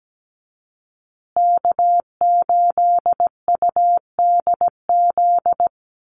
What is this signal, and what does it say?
Morse code, heard as a keyed tone, about 17 wpm, 700 Hz. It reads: K8UDZ
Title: USA Call Signs Repeated 3x 17wpm